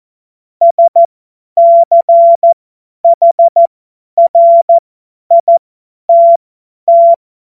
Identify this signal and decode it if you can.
Morse code, heard as a keyed tone, about 14 wpm, spaced as for 10 wpm, 680 Hz. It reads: SCHRITT
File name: Morsekurs / Witze / Witz 17 Juni 2025 50 F70